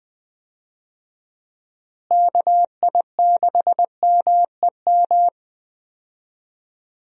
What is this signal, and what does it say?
Morse code, heard as a keyed tone, about 20 wpm, 700 Hz. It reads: KI6MEM
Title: USA Call Signs - 6 Characters - Repeated 3x - 20wpm